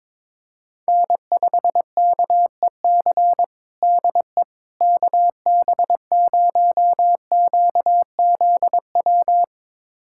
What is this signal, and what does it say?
Morse code, heard as a keyed tone, about 21 wpm, 700 Hz. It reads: N5KEC DE KB0QZW
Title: USA Callsign DE Callsign 22wpm